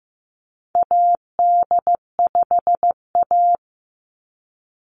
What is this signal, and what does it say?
Morse code, heard as a keyed tone, about 15 wpm, 700 Hz. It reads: AD5A
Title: NA SOTA Activator Callsigns 15wpm